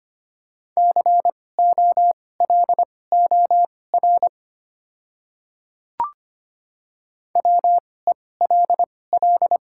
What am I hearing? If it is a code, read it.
Morse code, heard as a keyed tone, about 25 wpm, spaced as for 18 wpm, 700 Hz. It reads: COLOR WELL
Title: 2x Character Spacing - Top 500 Words - 25wpm